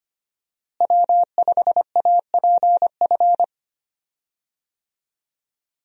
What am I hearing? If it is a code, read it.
Morse code, heard as a keyed tone, about 25 wpm, 700 Hz. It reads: W5APF
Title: USA Call Signs - 5 Characters - Repeated 3x - 25wpm